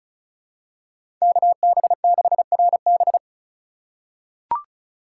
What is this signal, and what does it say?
Morse code, heard as a keyed tone, about 36 wpm, 700 Hz. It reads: KB6RB
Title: USA Call Signs - 5 Characters - 4x Character Spacing - 35wpm